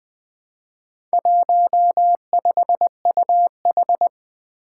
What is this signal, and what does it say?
Morse code, heard as a keyed tone, about 20 wpm, 700 Hz. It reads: 15UH
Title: Four Challenging Letter-Number Combinations 20wpm